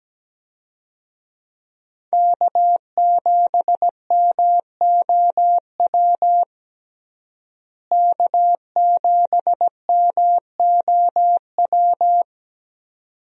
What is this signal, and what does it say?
Morse code, heard as a keyed tone, about 17 wpm, 700 Hz. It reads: K7MOW K7MOW
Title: USA Call Signs - 5 Characters - Repeated 2x - 17wpm